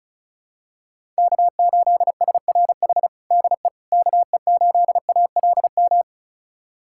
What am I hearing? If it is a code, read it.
Morse code, heard as a keyed tone, about 34 wpm, 700 Hz. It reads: K8SRH DE KE8ALM
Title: USA Callsign DE Callsign 35wpm